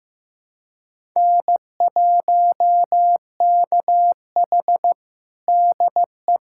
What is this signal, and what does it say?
Morse code, heard as a keyed tone, about 15 wpm, 700 Hz. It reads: N1KH DE
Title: USA Callsign DE Callsign - 4 Characters - 15wpm